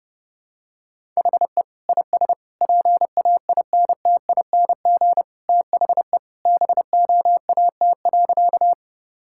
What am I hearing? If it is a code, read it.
Morse code, heard as a keyed tone, about 30 wpm, 700 Hz. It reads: HE IS PAINTING THE BOAT.
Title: Sentences from Top 1000 Words - Review Part 6 - 30wpm